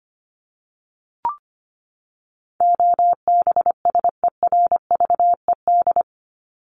Morse code, heard as a keyed tone, about 25 wpm, 700 Hz. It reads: OBSERVED
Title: ED Suffix - Top 100 Words with Base Words 25wpm